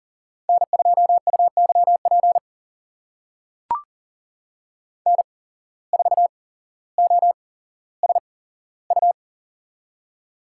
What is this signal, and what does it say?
Morse code, heard as a keyed tone, about 37 wpm, spaced as for 17 wpm, 700 Hz. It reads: N2UYP N4OSU
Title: USA Call Signs - 5 Characters - 8x Character Spacing - 40wpm